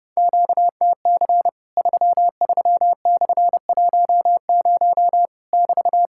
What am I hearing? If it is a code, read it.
Morse code, heard as a keyed tone, about 32 wpm, 700 Hz. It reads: QTC 33/10 <BT>
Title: Worked All Europe DX Contest - QTC Exchange - 30wpm